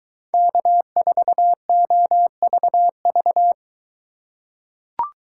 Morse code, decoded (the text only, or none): K4OVV